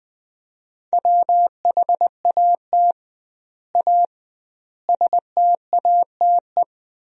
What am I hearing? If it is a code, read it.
Morse code, heard as a keyed tone, about 20 wpm, 700 Hz. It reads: WHAT A STATE
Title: 2x Word-Spacing - Sentences from Top 400 Words - 20wpm